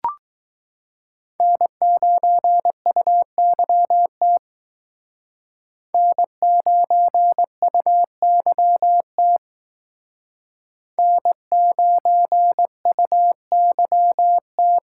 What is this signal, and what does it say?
Morse code, heard as a keyed tone, about 19 wpm, 700 Hz. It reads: N9UYT N9UYT N9UYT